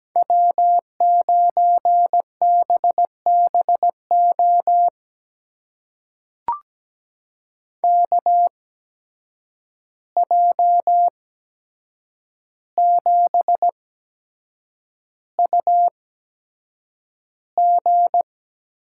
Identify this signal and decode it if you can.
Morse code, heard as a keyed tone, about 17 wpm, spaced as for 7 wpm, 700 Hz. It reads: W9BBO KJ7UG
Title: USA Call Signs - 5 Characters - 8x Character Spacing - 17wpm